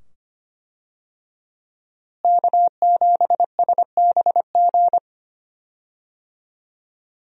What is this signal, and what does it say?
Morse code, heard as a keyed tone, about 25 wpm, 700 Hz. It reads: K7SBG
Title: USA Call Signs - 5 Characters - Repeated 2x - 25wpm